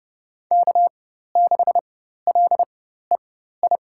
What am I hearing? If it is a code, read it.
Morse code, heard as a keyed tone, about 32 wpm, spaced as for 14 wpm, 700 Hz. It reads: K6LEI
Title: USA Call Signs - 5 Characters - 4x Character Spacing - 30wpm